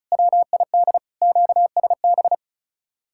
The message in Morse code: WID QSB

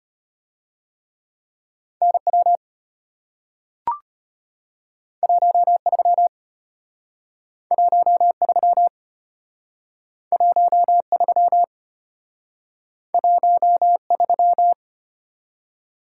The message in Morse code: NW 13 13 13 13

